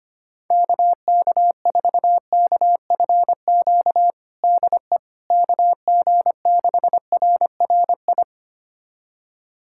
Morse code, heard as a keyed tone, about 25 wpm, 700 Hz. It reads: KK4KFQ DE KG6RRI
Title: USA Callsign DE Callsign - 6 Characters - 25wpm